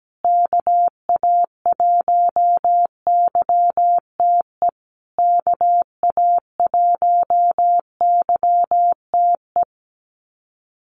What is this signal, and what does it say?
Morse code, heard as a keyed tone, about 17 wpm, 700 Hz. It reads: KA1YTE KA1YTE
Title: USA Call Signs Repeated 3x 17wpm